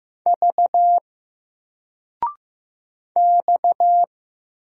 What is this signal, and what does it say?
Morse code, heard as a keyed tone, about 15 wpm, 700 Hz. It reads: V X
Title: Single Letter-Number 15wpm